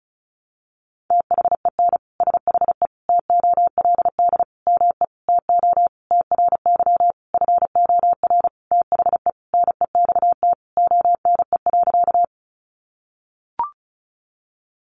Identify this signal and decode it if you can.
Morse code, heard as a keyed tone, about 34 wpm, 700 Hz. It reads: THEN SHE TOLD ME TO TRY FOR THE NEXT ONE.
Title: Sentences from Top 400 Words 35wpm